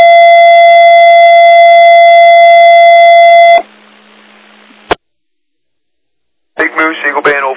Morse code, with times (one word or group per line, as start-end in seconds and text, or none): none